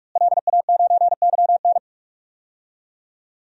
Morse code, heard as a keyed tone, about 46 wpm, 700 Hz. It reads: LA9YN
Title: DXCC-100 Call Signs - 5 Characters - Repeated 2x - 45wpm